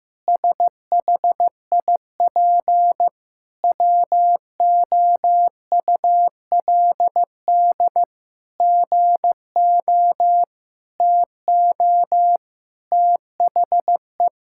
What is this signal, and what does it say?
Morse code, heard as a keyed tone, about 16 wpm, 700 Hz. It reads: SHIP WOULD GO TO THE
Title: Sentences from Top 1000 Words - Review Part 6 - 15wpm